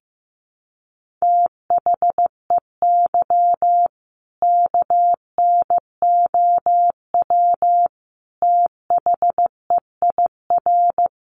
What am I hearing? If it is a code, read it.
Morse code, heard as a keyed tone, about 15 wpm, 700 Hz. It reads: THEY KNOW THEIR
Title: Sentences from Top 1200 Words 15wpm